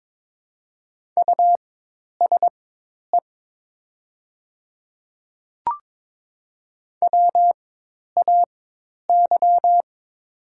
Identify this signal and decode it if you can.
Morse code, heard as a keyed tone, about 22 wpm, spaced as for 10 wpm, 700 Hz. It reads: USE WAY
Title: 4x Character Spacing - Top 200 Words - 22wpm